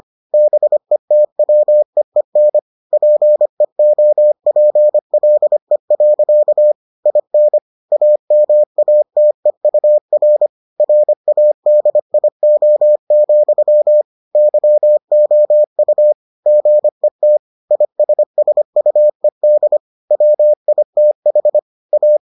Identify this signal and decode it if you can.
Morse code, heard as a keyed tone, about 25 wpm, 600 Hz. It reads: BETWEEN PEOPLE. IN AMATEUR RADIO, YOU GET ISSUED WITH A